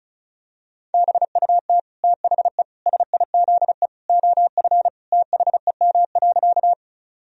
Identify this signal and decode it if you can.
Morse code, heard as a keyed tone, about 34 wpm, 700 Hz. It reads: BUT THE SIZE OF THEM.
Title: Sentences from Top 1000 Words - Review Part 2 - 35wpm